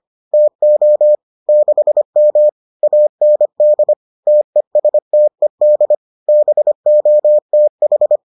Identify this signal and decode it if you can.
Morse code, heard as a keyed tone, about 25 wpm, 600 Hz. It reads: TO 6M AND TESTED BOTH